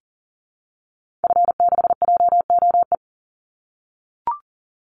Morse code, heard as a keyed tone, about 40 wpm, 700 Hz. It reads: F6JOE
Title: Call Signs - 40wpm